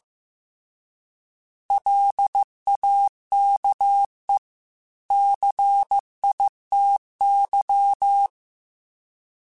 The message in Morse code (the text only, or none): LAKE CITY